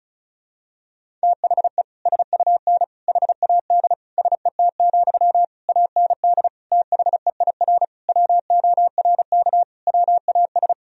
Morse code, heard as a keyed tone, about 36 wpm, 700 Hz. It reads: THE SUN HAD SET, AND THEIR WORK WAS
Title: Sentences from Top 1000 Words - Review Part 1 - 35wpm